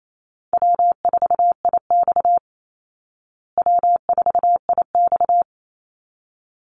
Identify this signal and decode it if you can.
Morse code, heard as a keyed tone, about 28 wpm, 700 Hz. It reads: W4IX W4IX
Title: Easy Call Signs Repeated 2x - 28wpm